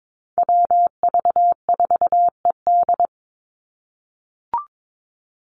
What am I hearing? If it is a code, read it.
Morse code, heard as a keyed tone, about 22 wpm, 700 Hz. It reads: WV4ED
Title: USA Call Signs 22wpm